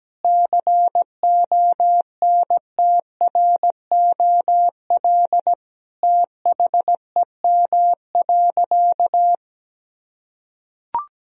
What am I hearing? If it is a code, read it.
Morse code, heard as a keyed tone, about 17 wpm, 700 Hz. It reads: CONTROL THEM.
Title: Sentences from Top 1000 Words - Review Part 2 - 17wpm